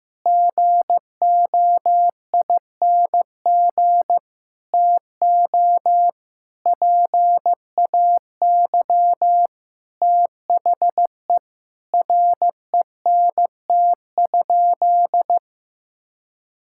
Morse code, heard as a keyed tone, about 15 wpm, 700 Hz. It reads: GOING TO PAY THE RENT?
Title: Sentences from Top 1000 Words - Review Part 5 - 15wpm